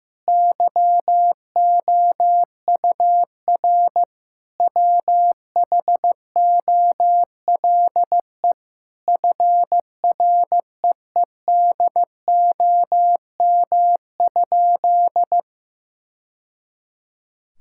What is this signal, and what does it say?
Morse code, heard as a keyed tone, about 15 wpm, 700 Hz. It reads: YOUR WHOLE FREEDOM?
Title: Sentences from Top 1000 Words - Review Part 5 - 15wpm